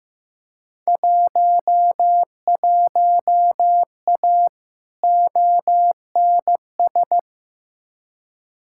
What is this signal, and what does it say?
Morse code, heard as a keyed tone, about 15 wpm, 700 Hz. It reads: 11A ONS
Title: Field Day Exchanges Encore - 15wpm